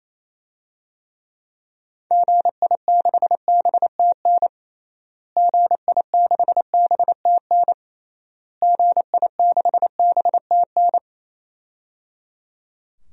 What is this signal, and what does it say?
Morse code, heard as a keyed tone, about 28 wpm, 700 Hz. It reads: GI6BTN GI6BTN GI6BTN
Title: DXCC-100 Call Signs - 6 Characters - Repeated 3x - 28wpm